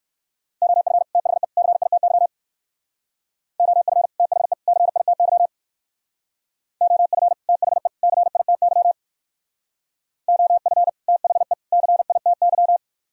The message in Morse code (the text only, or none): OF THE CITY OF THE CITY OF THE CITY OF THE CITY